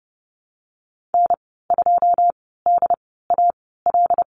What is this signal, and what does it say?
Morse code, heard as a keyed tone, about 30 wpm, spaced as for 17 wpm, 700 Hz. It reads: N2DAL
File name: Lesson 14g - Number 2 - Call Signs 30wpm